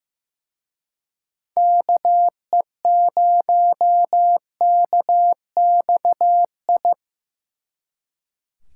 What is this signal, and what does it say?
Morse code, heard as a keyed tone, about 18 wpm, 700 Hz. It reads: KE0KXI
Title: Speed Racing - USA 2x3 Call Signs 15wpm